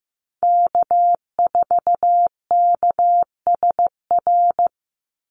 Morse code, heard as a keyed tone, about 15 wpm, 700 Hz. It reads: K4KSR
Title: Call Signs Repeated 3x - 15wpm